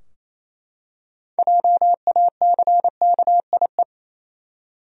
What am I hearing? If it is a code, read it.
Morse code, heard as a keyed tone, about 28 wpm, 700 Hz. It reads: JACKIE
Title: USA Common Names - 28wpm